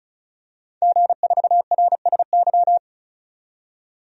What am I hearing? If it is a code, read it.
Morse code, heard as a keyed tone, about 36 wpm, 700 Hz. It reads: G4RSY
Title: DXCC-100 Call Signs - 5 Characters - 35wpm